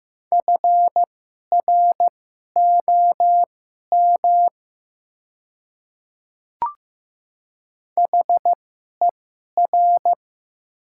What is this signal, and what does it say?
Morse code, heard as a keyed tone, about 15 wpm, spaced as for 11 wpm, 700 Hz. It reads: FROM HER
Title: 2x Character Spacing - Top 100 Words - 15wpm